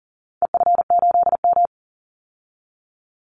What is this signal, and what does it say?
Morse code, heard as a keyed tone, about 40 wpm, 700 Hz. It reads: EF8M
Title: Call Signs Repeated 2x - 40wpm